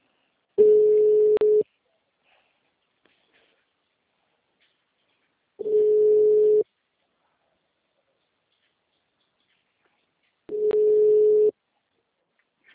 Ringback tone (Europe, 425 Hz).